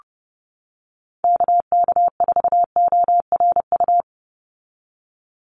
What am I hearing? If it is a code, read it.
Morse code, heard as a keyed tone, about 28 wpm, 700 Hz. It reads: KK4ORU
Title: USA Call Signs 30wpm